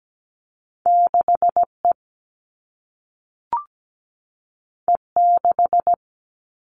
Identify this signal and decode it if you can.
Morse code, heard as a keyed tone, about 17 wpm, 700 Hz. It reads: 6E E6
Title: All Cut Numbers - 2-Digit 17wpm